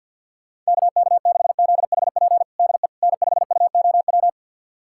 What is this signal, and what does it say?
Morse code, heard as a keyed tone, about 51 wpm, 700 Hz. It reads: KK6ZHP DE N5UOW
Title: USA Callsign DE Callsign 50wpm